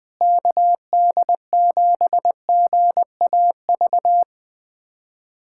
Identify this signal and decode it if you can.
Morse code, heard as a keyed tone, about 20 wpm, 700 Hz. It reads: KD7GAV
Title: USA Callsign DE Callsign - 6 Characters - 20wpm